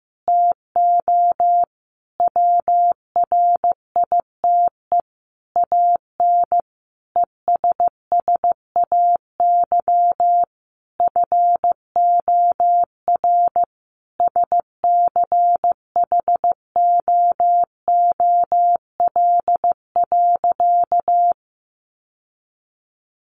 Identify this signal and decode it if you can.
Morse code, heard as a keyed tone, about 15 wpm, 700 Hz. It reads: TO WRITE AN ESSAY FOR SCHOOL.
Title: Sentences from Top 1500 Words 15wpm